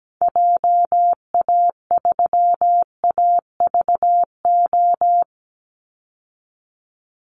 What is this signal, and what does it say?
Morse code, heard as a keyed tone, about 17 wpm, 700 Hz. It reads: JA3AVO
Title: Call Signs Repeated 3x - 17wpm